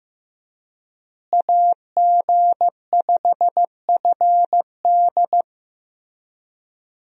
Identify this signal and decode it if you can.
Morse code, heard as a keyed tone, about 15 wpm, 700 Hz. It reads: AG5FD
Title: USA Call Signs - 5 Characters - Repeated 2x - 15wpm